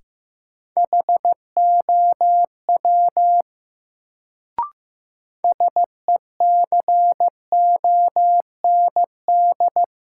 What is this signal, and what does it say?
Morse code, heard as a keyed tone, about 15 wpm, 700 Hz. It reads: HOW SECOND
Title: Sets of 2 Words - New Word Drill - 15wpm